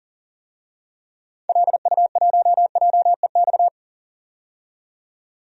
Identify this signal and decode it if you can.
Morse code, heard as a keyed tone, about 40 wpm, 700 Hz. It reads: LU1JEX
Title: DXCC-100 Call Signs - 6 Characters - 40wpm